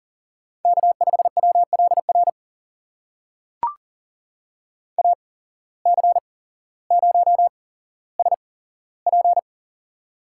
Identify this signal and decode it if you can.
Morse code, heard as a keyed tone, about 41 wpm, spaced as for 17 wpm, 700 Hz. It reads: K5WLR AC0SP
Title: USA Call Signs - 5 Characters - 8x Character Spacing - 40wpm